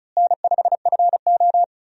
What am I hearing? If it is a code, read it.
Morse code, heard as a keyed tone, about 36 wpm, 700 Hz. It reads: N5FO
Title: USA Call Signs - 4 Characters - Repeated 2x - 35wpm